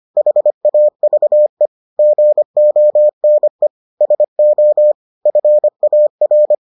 Morse code, heard as a keyed tone, about 25 wpm, 600 Hz. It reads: HAVE GONE SO FAR